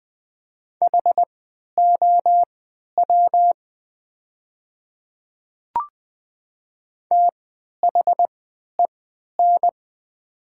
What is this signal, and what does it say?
Morse code, heard as a keyed tone, about 20 wpm, spaced as for 11 wpm, 700 Hz. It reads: HOW THEN
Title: 3x Character Spacing - Top 100 Words - 20wpm